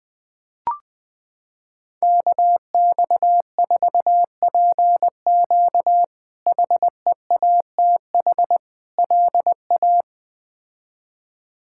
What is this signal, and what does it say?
Morse code, heard as a keyed tone, about 20 wpm, 700 Hz. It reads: KX4PQ HEATH LA
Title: K1USN SST QSO Practice - 20wpm